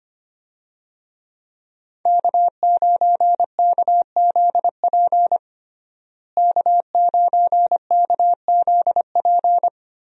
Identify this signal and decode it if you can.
Morse code, heard as a keyed tone, about 25 wpm, 700 Hz. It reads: K9KZP K9KZP
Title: USA Call Signs - 5 Characters - Repeated 3x - 25wpm